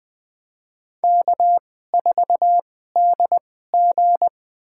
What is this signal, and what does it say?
Morse code, heard as a keyed tone, about 20 wpm, spaced as for 14 wpm, 700 Hz. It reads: K4DG
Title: USA Call Signs - 4 Characters - 2x Character Spacing - 20wpm